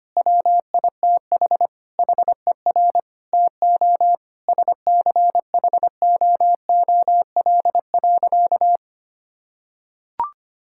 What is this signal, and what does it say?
Morse code, heard as a keyed tone, about 25 wpm, 700 Hz. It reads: WITH HER TO SCHOOL.
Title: Sentences from Top 1000 Words - Review Part 5 - 25wpm